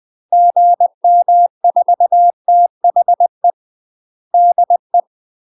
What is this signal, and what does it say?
Morse code, heard as a keyed tone, about 20 wpm, 700 Hz. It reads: GM4THE DE
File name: UK MASTER QSOs TRAINING/20WPM